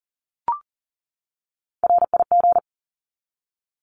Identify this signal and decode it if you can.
Morse code, heard as a keyed tone, about 40 wpm, 700 Hz. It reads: RIG
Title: Abbreviations 40wpm